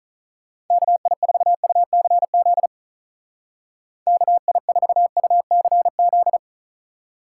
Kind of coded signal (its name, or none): Morse code